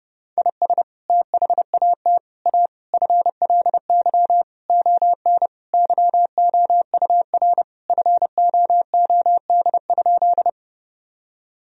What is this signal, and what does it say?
Morse code, heard as a keyed tone, about 32 wpm, 700 Hz. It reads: IS THAT A FLY ON YOUR FOOD?
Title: Sentences from Top 1000 Words - Review Part 2 - 30wpm